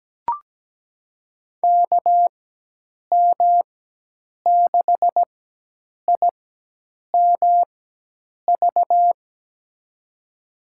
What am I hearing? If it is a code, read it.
Morse code, heard as a keyed tone, about 17 wpm, spaced as for 8 wpm, 700 Hz. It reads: KM6IMV
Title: USA Call Signs - 6 Characters - 4x Character Spacing - 17wpm